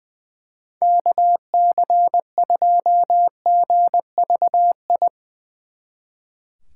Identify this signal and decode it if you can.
Morse code, heard as a keyed tone, about 22 wpm, 700 Hz. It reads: KC2GVI
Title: Speed Racing - USA 2x3 Call Signs 20wpm